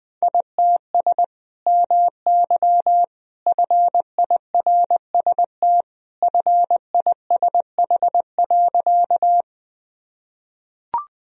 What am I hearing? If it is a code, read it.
Morse code, heard as a keyed tone, about 20 wpm, 700 Hz. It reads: ITS MY FIRST FISH.
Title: Sentences from Top 1000 Words - Review Part 1 - 20wpm